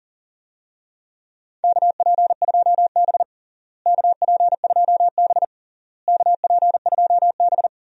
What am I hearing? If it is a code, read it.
Morse code, heard as a keyed tone, about 37 wpm, 700 Hz. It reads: KP2B KP2B KP2B
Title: DXCC-100 Call Signs - 4 Characters - Repeated 3x - 40wpm